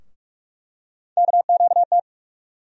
Morse code, heard as a keyed tone, about 46 wpm, 700 Hz. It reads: KQT
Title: USA Call Sign Postfixes - 3 Letters 45wpm